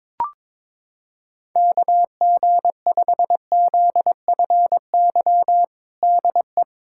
Morse code, heard as a keyed tone, about 21 wpm, 700 Hz. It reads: KG5ZFY DE
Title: USA Callsign DE Callsign - 6 Characters - 22wpm